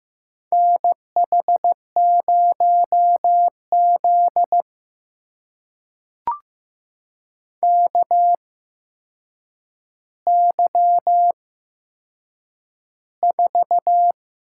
Morse code, heard as a keyed tone, about 15 wpm, spaced as for 7 wpm, 700 Hz. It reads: NH0Z KY4
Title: USA Call Signs - 4 Characters - 8x Character Spacing - 15wpm